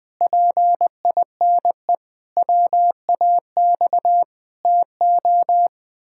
Morse code, heard as a keyed tone, about 20 wpm, 700 Hz. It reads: PINE WAX TO
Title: Pangram Challenge v2 - 20wpm